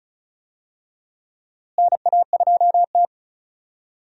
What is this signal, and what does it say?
Morse code, heard as a keyed tone, about 36 wpm, 700 Hz. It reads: NA2T
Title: USA Call Signs - 4 Characters - Repeated 2x - 35wpm